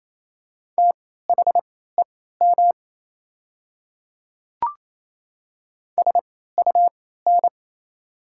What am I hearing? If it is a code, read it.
Morse code, heard as a keyed tone, about 28 wpm, spaced as for 16 wpm, 690 Hz. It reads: THEM SUN
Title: 3x Character Spacing - Top 500 Words - 28wpm